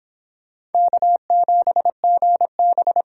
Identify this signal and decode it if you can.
Morse code, heard as a keyed tone, about 24 wpm, 700 Hz. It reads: K7GB